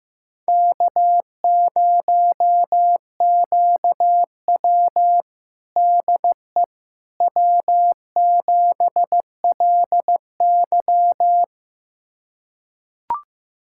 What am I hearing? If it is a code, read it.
Morse code, heard as a keyed tone, about 15 wpm, 700 Hz. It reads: K0QW DE W7LY
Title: USA Callsign DE Callsign - 4 Characters - 15wpm